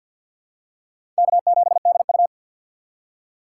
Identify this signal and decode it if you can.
Morse code, heard as a keyed tone, about 51 wpm, 700 Hz. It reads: K7DU
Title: USA Call Signs - 4 Characters - 3x Character Spacing - 50wpm